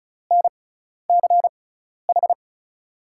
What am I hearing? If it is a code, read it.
Morse code, heard as a keyed tone, about 36 wpm, spaced as for 12 wpm, 700 Hz. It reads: NCH
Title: USA Call Signs - 5 Characters - 6x Character Spacing - 35wpm